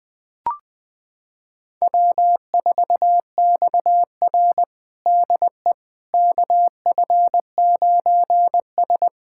Morse code, heard as a keyed tone, about 20 wpm, 700 Hz. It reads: W4XR DE KF9S
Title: USA Callsign DE Callsign - 4 Characters - 20wpm